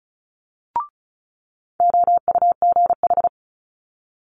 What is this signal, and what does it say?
Morse code, heard as a keyed tone, about 36 wpm, 700 Hz. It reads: OUGH
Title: Multi-letter Phonemes Quick Fire 35wpm